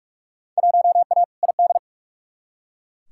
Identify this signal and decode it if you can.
Morse code, heard as a keyed tone, about 42 wpm, 700 Hz. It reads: 1A ID
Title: Field Day Exchanges - 45wpm